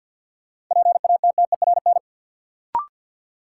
Morse code, heard as a keyed tone, about 53 wpm, 700 Hz. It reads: PATTERN
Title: Sentences from Top 1000 Words - New Word Drill - 50wpm